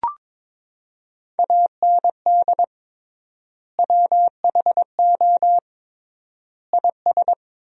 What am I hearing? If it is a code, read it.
Morse code, heard as a keyed tone, about 22 wpm, 700 Hz. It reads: AND WHO IS
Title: 3x Word-Spacing - Sentences from Top 300 Words - 22wpm